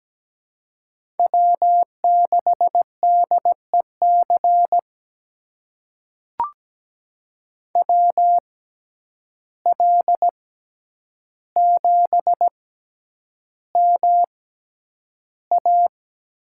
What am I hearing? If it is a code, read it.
Morse code, heard as a keyed tone, about 17 wpm, spaced as for 9 wpm, 700 Hz. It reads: W6DEC WL7MA
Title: USA Call Signs - 5 Characters - 6x Character Spacing - 17wpm